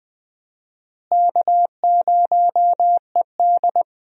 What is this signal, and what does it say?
Morse code, heard as a keyed tone, about 20 wpm, 700 Hz. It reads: K0ED
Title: USA Call Signs - 4 Characters - Repeated 3x - 20wpm